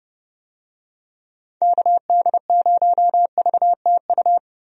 Morse code, heard as a keyed tone, about 30 wpm, 700 Hz. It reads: KD0VTU
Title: USA Call Signs - 6 Characters - Repeated 3x - 30wpm